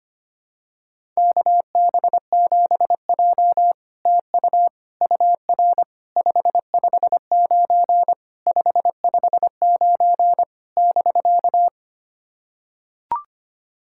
Morse code, heard as a keyed tone, about 25 wpm, 700 Hz. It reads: KB7J TU UR 559 559 <BK>
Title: SOTA QSO Practice 25wpm